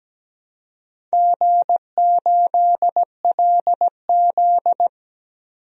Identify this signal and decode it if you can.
Morse code, heard as a keyed tone, about 17 wpm, 700 Hz. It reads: G8LZ
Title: DXCC-100 Call Signs - 4 Characters - 17wpm